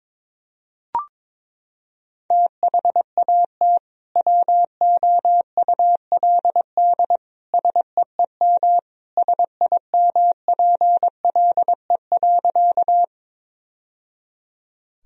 Morse code, heard as a keyed tone, about 22 wpm, 700 Hz. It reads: THAT WOULD SEEM SIMPLE.
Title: Sentences from Top 1000 Words - Review Part 1 - 22wpm